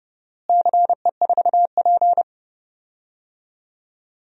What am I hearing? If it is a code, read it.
Morse code, heard as a keyed tone, about 30 wpm, 700 Hz. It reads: CE4P
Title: DXCC-100 Call Signs - 4 Characters - Repeated 2x - 30wpm